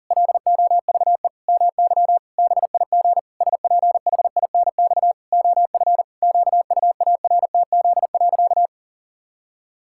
Morse code, heard as a keyed tone, about 40 wpm, 700 Hz. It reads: LOVE MY BIG SPHINX OF QUARTZ.
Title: Pangram Challenge v2 - 40wpm